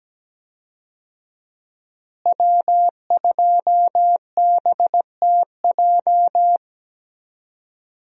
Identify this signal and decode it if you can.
Morse code, heard as a keyed tone, about 17 wpm, 700 Hz. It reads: W2BTJ
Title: USA Call Signs - 5 Characters - Repeated 2x - 17wpm